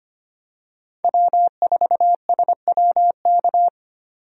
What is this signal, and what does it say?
Morse code, heard as a keyed tone, about 25 wpm, 700 Hz. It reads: W4SWK
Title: USA Call Signs - 5 Characters - 25wpm